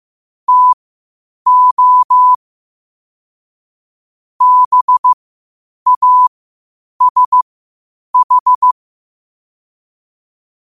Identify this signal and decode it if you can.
Morse code, heard as a keyed tone, about 15 wpm, spaced as for 8 wpm, 1000 Hz. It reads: TO BASH
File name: qotdfiles / 8wpm